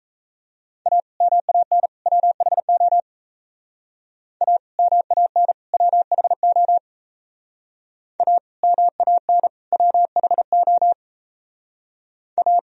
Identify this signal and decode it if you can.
Morse code, heard as a keyed tone, about 33 wpm, 700 Hz. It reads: A MAN WHO A MAN WHO A MAN WHO A
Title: Speed-Racing - Sets of 3 Words 28wpm